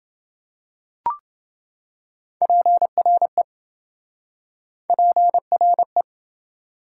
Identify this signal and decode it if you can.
Morse code, heard as a keyed tone, about 27 wpm, 700 Hz. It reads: PRE PRE